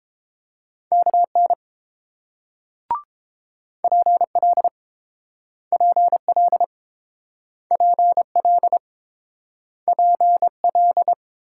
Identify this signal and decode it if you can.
Morse code, heard as a keyed tone, about 30 wpm, 700 Hz. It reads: KN PL PL PL PL